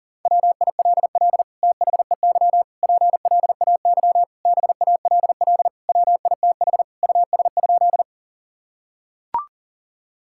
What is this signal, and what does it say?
Morse code, heard as a keyed tone, about 40 wpm, 700 Hz. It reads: WILL THEY PLAY BALL WITH US?
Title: Sentences from Top 1000 Words - Review Part 6 - 40wpm